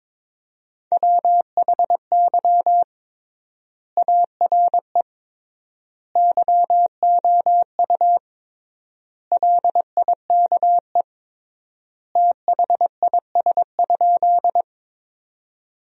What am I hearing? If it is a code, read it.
Morse code, heard as a keyed tone, about 22 wpm, 700 Hz. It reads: WHY ARE YOU LIKE THIS?
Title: 3x Word-Spacing - Sentences from Top 200 Words - 22wpm